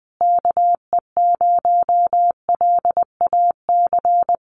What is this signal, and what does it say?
Morse code, heard as a keyed tone, about 20 wpm, 700 Hz. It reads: KE0LAC
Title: USA Call Signs Repeated 3x 20wpm